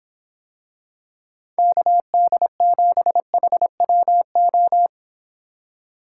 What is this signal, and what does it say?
Morse code, heard as a keyed tone, about 26 wpm, 700 Hz. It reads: KD7HWO